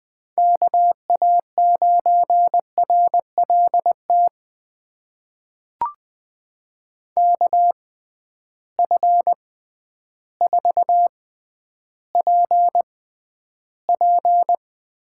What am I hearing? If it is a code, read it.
Morse code, heard as a keyed tone, about 20 wpm, spaced as for 11 wpm, 700 Hz. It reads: KA9RLT KF4PP
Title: USA Call Signs - 6x Character Spacing - 20wpm